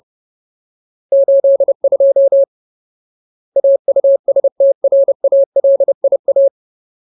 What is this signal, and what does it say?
Morse code, heard as a keyed tone, about 30 wpm, 550 Hz. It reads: 82 AUSTRALIA
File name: Ultimate Word List / 30WPM Ultimate